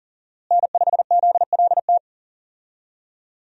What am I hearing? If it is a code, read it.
Morse code, heard as a keyed tone, about 40 wpm, 700 Hz. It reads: N5ZLT
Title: USA Call Signs - 5 Characters - Repeated 2x - 40wpm